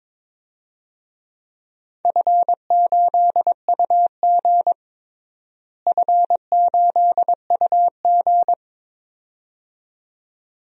Morse code, heard as a keyed tone, about 22 wpm, 700 Hz. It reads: F8UG F8UG
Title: DXCC-100 Call Signs - 4 Characters - Repeated 2x - 22wpm